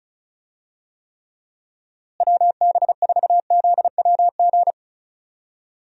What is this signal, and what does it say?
Morse code, heard as a keyed tone, about 36 wpm, 700 Hz. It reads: WB4ZWG
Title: USA Call Signs - 6 Characters - Repeated 3x - 35wpm